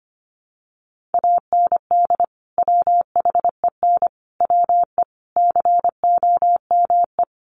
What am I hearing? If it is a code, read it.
Morse code, heard as a keyed tone, about 25 wpm, 700 Hz. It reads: AND WHEN WE COME
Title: Sets of 5 Words 25wpm